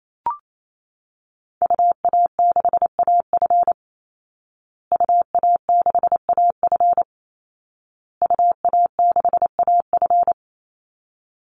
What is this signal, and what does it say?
Morse code, heard as a keyed tone, about 28 wpm, 700 Hz. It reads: UA6AF UA6AF UA6AF
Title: Call Signs Repeated 2x - 28wpm